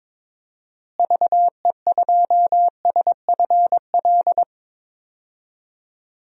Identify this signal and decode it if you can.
Morse code, heard as a keyed tone, about 22 wpm, 700 Hz. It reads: VE2SFL
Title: DXCC-100 Call Signs - 6 Characters - Repeated 3x - 22wpm